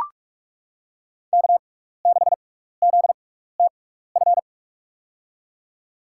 Morse code, heard as a keyed tone, about 41 wpm, spaced as for 15 wpm, 690 Hz. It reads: K6ZTF
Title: USA Call Signs - 5 Characters - 6x Character Spacing - 45wpm